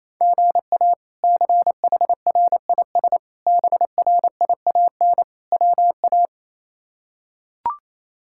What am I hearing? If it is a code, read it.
Morse code, heard as a keyed tone, about 28 wpm, 700 Hz. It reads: GA CHRIS BRIAN WA
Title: K1USN SST QSO Practice - 28wpm